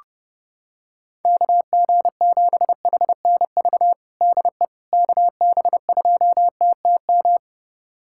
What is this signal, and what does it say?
Morse code, heard as a keyed tone, about 30 wpm, 700 Hz. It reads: KG7HNV DE KB2TTM
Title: USA Callsign DE Callsign - 6 Characters - 30wpm